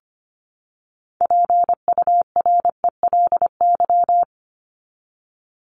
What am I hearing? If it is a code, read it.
Morse code, heard as a keyed tone, about 25 wpm, 700 Hz. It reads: PURELY
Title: LY Suffix - Top 100 Words with Base Words 25wpm